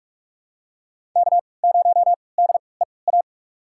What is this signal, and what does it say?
Morse code, heard as a keyed tone, about 43 wpm, spaced as for 25 wpm, 700 Hz. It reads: K0DEA
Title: USA Call Signs - 5 Characters - 3x Character Spacing - 45wpm